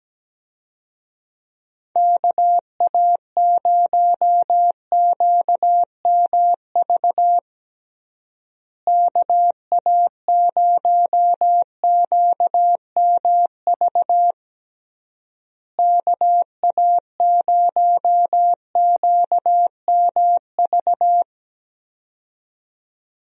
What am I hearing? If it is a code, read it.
Morse code, heard as a keyed tone, about 17 wpm, 700 Hz. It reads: KA0QMV KA0QMV KA0QMV
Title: USA Call Signs - 6 Characters - Repeated 3x - 17wpm